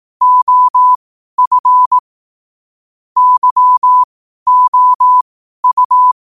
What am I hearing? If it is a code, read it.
Morse code, heard as a keyed tone, about 18 wpm, spaced as for 12 wpm, 1000 Hz. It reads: OF YOU
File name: qotdfiles / 12wpm